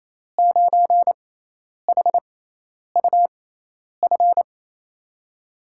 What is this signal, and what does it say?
Morse code, heard as a keyed tone, about 28 wpm, spaced as for 10 wpm, 700 Hz. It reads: 9HUF
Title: USA Call Signs - 5 Characters - 6x Character Spacing - 28wpm